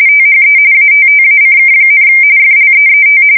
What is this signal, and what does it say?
RTTY, about 45.45 baud. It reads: THE ATTACK, FLEET